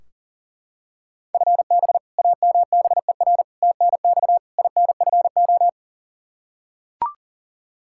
Morse code, heard as a keyed tone, about 40 wpm, 700 Hz. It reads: FB AMBER TNX INFO
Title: Practice QSO - Exchange 3 - More Info - 40wpm